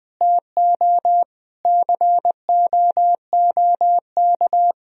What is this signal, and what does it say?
Morse code, heard as a keyed tone, about 20 wpm, 700 Hz. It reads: TO COOK
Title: Sentences from Top 1000 Words - Review Part 4 - 20wpm